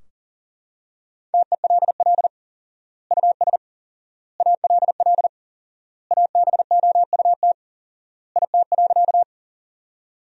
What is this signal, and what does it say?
Morse code, heard as a keyed tone, about 37 wpm, 700 Hz. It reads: TELL US ALL ABOUT IT.
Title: 4x Word-Spacing - Sentences from Top 200 Words - 40wpm